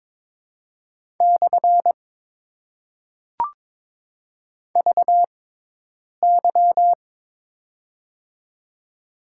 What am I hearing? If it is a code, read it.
Morse code, heard as a keyed tone, about 23 wpm, 700 Hz. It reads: / V Y
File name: Top 90 QSO Elements - 6x Character Spacing - 22wpm